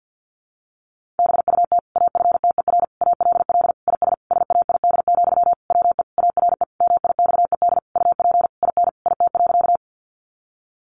Morse code, heard as a keyed tone, about 50 wpm, 700 Hz. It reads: BUT AFTER ALL IS SAID, WE ARE MIXED UP IN IT.
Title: Sentences from Top 1500 Words 50wpm